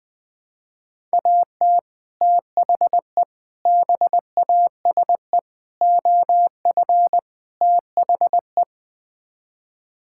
Morse code, heard as a keyed tone, about 20 wpm, 700 Hz. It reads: AT THE BASE OF THE
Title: Build A Phrase - 5-Word Phrases - 20wpm